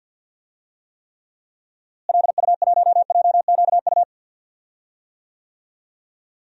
Morse code, heard as a keyed tone, about 53 wpm, 700 Hz. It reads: LU1JQU
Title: DXCC-100 Call Signs - 6 Characters - Repeated 3x - 50wpm